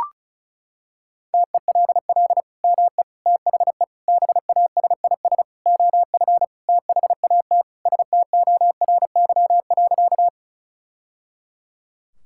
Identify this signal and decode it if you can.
Morse code, heard as a keyed tone, about 34 wpm, 700 Hz. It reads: TELL ME THE BASIS OF THAT STORY.
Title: Sentences from Top 1000 Words - Review Part 5 - 35wpm